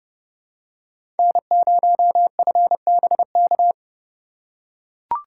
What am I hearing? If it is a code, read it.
Morse code, heard as a keyed tone, about 32 wpm, 700 Hz. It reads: N0FBK
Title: USA Call Signs - 3x Character Spacing - 30wpm